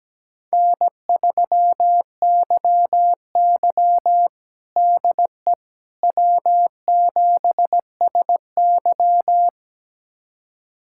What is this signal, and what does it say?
Morse code, heard as a keyed tone, about 17 wpm, 700 Hz. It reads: N3YY DE W7SY
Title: USA Callsign DE Callsign - 4 Characters - 17wpm